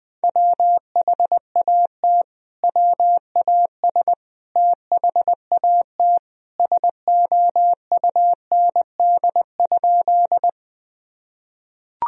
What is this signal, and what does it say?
Morse code, heard as a keyed tone, about 20 wpm, 700 Hz. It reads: WHAT WAS THAT SOUND?
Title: Sentences from Top 1000 Words - Review Part 1 - 20wpm